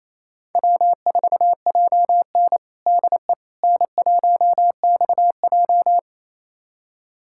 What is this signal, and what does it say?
Morse code, heard as a keyed tone, about 28 wpm, 700 Hz. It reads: W4JN DE N1XJ
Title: USA Callsign DE Callsign - 4 Characters - 28wpm